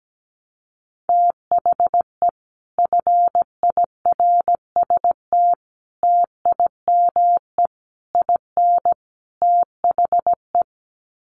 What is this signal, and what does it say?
Morse code, heard as a keyed tone, about 17 wpm, 700 Hz. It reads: THE FIRST TIME IN THE
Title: Sets of 5 Words Encore 17wpm